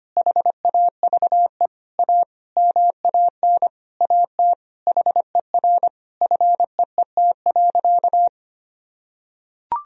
Morse code, heard as a keyed tone, about 25 wpm, 700 Hz. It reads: HAVE A MAN AT HER FEET.
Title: Sentences from Top 1000 Words - Review Part 1 - 25wpm